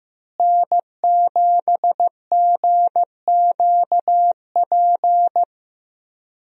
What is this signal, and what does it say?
Morse code, heard as a keyed tone, about 15 wpm, 700 Hz. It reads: N7GQP
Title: USA Call Signs - 5 Characters - Repeated 3x - 15wpm